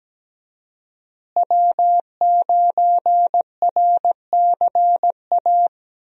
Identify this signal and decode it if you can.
Morse code, heard as a keyed tone, about 17 wpm, 700 Hz. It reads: W9RCA
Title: USA Call Signs - 5 Characters - Repeated 2x - 17wpm